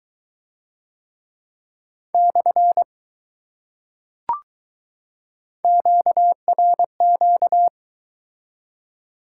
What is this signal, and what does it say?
Morse code, heard as a keyed tone, about 20 wpm, 700 Hz. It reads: / QRQ